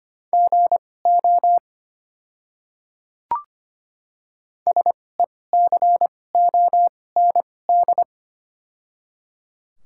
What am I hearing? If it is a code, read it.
Morse code, heard as a keyed tone, about 25 wpm, spaced as for 18 wpm, 700 Hz. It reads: GO SECOND
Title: 2x Character Spacing - Top 300 Words - 25wpm